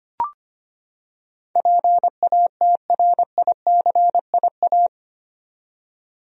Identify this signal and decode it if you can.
Morse code, heard as a keyed tone, about 25 wpm, 700 Hz. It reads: PATRICIA
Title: USA Common Names - 25wpm